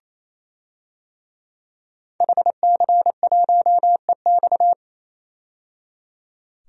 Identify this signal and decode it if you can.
Morse code, heard as a keyed tone, about 28 wpm, 700 Hz. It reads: HC1EX
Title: DXCC-100 Call Signs - 5 Characters - 28wpm